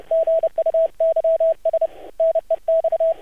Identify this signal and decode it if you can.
Morse code, heard as a keyed tone, about 30 wpm, 650 Hz, with radio noise. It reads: GUYS NEX
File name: archived / CRSnet